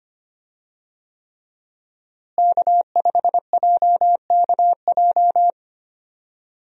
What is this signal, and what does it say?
Morse code, heard as a keyed tone, about 25 wpm, 700 Hz. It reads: K5JKJ
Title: USA Call Signs - 5 Characters - Repeated 3x - 25wpm